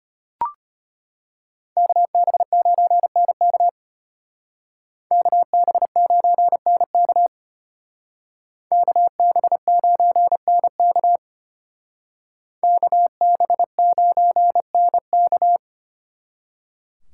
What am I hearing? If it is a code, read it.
Morse code, heard as a keyed tone, about 31 wpm, 700 Hz. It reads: KB9NK KB9NK KB9NK KB9NK
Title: Speed Racing - USA 2x2 Call Signs 25wpm